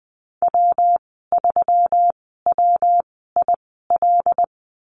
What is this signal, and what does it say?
Morse code, heard as a keyed tone, about 20 wpm, spaced as for 15 wpm, 700 Hz. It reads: W3WIL
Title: 122 Lesson 20g - Number 3 - Call Signs 20wpm